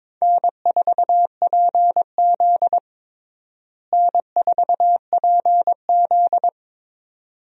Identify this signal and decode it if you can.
Morse code, heard as a keyed tone, about 22 wpm, 700 Hz. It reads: N4PZ N4PZ
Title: USA Call Signs - 4 Characters - Repeated 2x - 22wpm